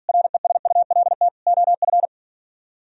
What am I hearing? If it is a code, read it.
Morse code, heard as a keyed tone, about 57 wpm, 700 Hz. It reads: RESULT OF